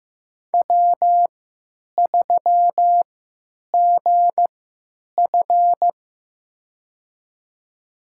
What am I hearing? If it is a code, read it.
Morse code, heard as a keyed tone, about 15 wpm, spaced as for 8 wpm, 700 Hz. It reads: W3GF
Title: USA Call Signs - 3x Character Spacing - 15wpm